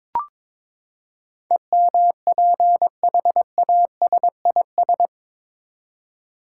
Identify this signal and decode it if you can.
Morse code, heard as a keyed tone, about 21 wpm, 700 Hz. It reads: EMPHASIS